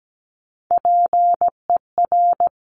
Morse code, heard as a keyed tone, about 17 wpm, 700 Hz. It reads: PER
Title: Top Three Letter Combinations in Words 17wpm